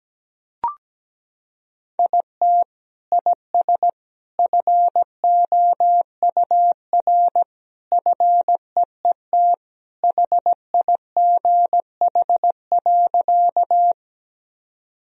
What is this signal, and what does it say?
Morse code, heard as a keyed tone, about 17 wpm, 700 Hz. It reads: IT IS FOUR FEET HIGH.
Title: Sentences from Top 1000 Words - Review Part 1 - 17wpm